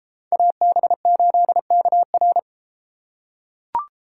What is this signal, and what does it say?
Morse code, heard as a keyed tone, about 30 wpm, 700 Hz. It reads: AB8KR